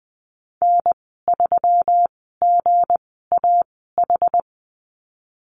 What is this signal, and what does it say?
Morse code, heard as a keyed tone, about 20 wpm, spaced as for 15 wpm, 700 Hz. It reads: N3GAH
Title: 161 Lesson 26g - Letter g - Call Signs 20wpm